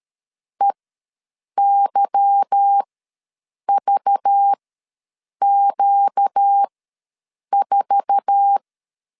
Morse code, heard as a keyed tone, about 13 wpm, spaced as for 7 wpm, 800 Hz. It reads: EYVQ4
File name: LEZIONI / 10 SIGNS / 10C DOT ws 12wpm 3548